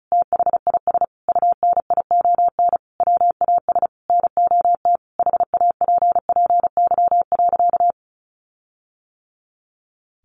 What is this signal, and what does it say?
Morse code, heard as a keyed tone, about 34 wpm, 700 Hz. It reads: THIS UNION WAS NOT HAPPY.
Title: Sentences from Top 1400 Words 35wpm